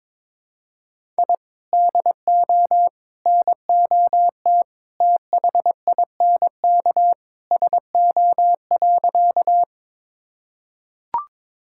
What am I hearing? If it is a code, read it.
Morse code, heard as a keyed tone, about 22 wpm, 700 Hz. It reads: I DO NOT THINK SO.
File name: practice / 22wpm / Sentences with 3-Word Phrases - 22wpm